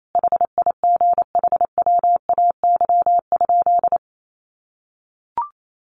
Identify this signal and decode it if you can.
Morse code, heard as a keyed tone, about 28 wpm, 700 Hz. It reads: HIGHWAY?
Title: Sentences from Top 1400 Words 28wpm